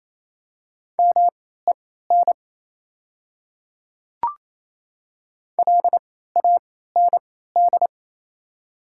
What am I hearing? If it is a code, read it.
Morse code, heard as a keyed tone, about 28 wpm, spaced as for 16 wpm, 700 Hz. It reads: MEN LAND
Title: 3x Character Spacing - Top 300 Words - 28wpm